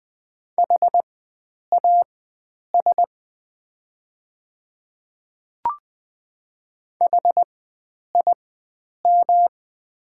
Morse code, heard as a keyed tone, about 20 wpm, spaced as for 9 wpm, 700 Hz. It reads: HAS HIM
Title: 4x Character Spacing - Top 100 Words - 20wpm